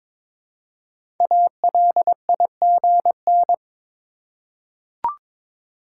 Morse code, heard as a keyed tone, about 23 wpm, 700 Hz. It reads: ALIGN
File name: 5 Letter Words - First 3 Letters Given - 22wpm